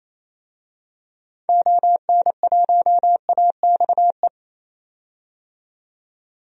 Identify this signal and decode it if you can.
Morse code, heard as a keyed tone, about 28 wpm, 700 Hz. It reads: ON1AXE
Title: DXCC-100 Call Signs - 6 Characters - 28wpm